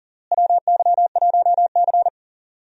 Morse code, heard as a keyed tone, about 40 wpm, 700 Hz. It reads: WY1C
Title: USA Call Signs - 4 Characters - Repeated 3x - 40wpm